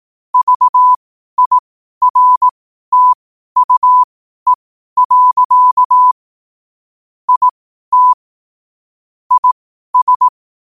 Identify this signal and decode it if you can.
Morse code, heard as a keyed tone, about 18 wpm, spaced as for 12 wpm, 1000 Hz. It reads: VIRTUE. IT IS
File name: qotdfiles / 12wpm